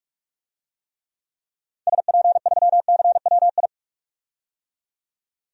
Morse code, heard as a keyed tone, about 46 wpm, 690 Hz. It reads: SP3CWI
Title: DXCC-100 Call Signs - 6 Characters - 45wpm